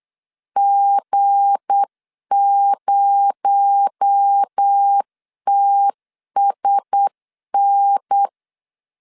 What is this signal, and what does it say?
Morse code, heard as a keyed tone, about 8 wpm, 800 Hz. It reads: G0TSN
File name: LEZIONI / 7 GF9Z / 7 G 8wpm 3118